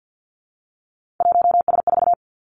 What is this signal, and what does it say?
Morse code, heard as a keyed tone, about 50 wpm, 700 Hz. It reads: 1S4
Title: Three Challenging Letter-Number Combinations 50wpm